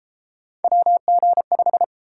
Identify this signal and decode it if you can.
Morse code, heard as a keyed tone, about 28 wpm, 700 Hz. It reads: WG5